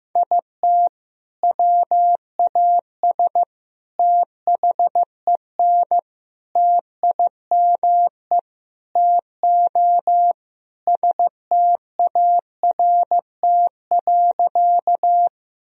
Morse code, heard as a keyed tone, about 15 wpm, 700 Hz. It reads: IT WAS THEN TIME TO START.
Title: Sentences from Top 1000 Words - Review Part 1 - 15wpm